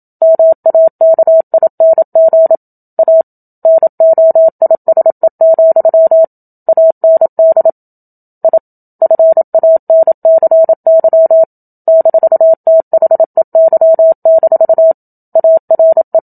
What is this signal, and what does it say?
Morse code, heard as a keyed tone, about 27 wpm, 650 Hz. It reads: MAKING A NOISE, AND I FANCY -THEY- ARE
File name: Book Two / 01-Chapter 1 - 27 WPM